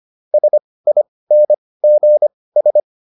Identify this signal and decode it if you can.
Morse code, heard as a keyed tone, about 25 wpm, spaced as for 18 wpm, 600 Hz. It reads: SINGS